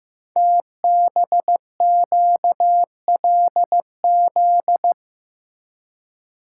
Morse code, heard as a keyed tone, about 15 wpm, 700 Hz. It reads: TBQLZ
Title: Code Groups - 5 Letter:Numbers - 15wpm